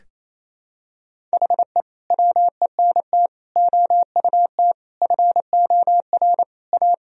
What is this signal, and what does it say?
Morse code, heard as a keyed tone, about 28 wpm, 700 Hz. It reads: HE WENT OUT FOR A
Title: Sentences from Top 1000 Words - Review Part 1 - 28wpm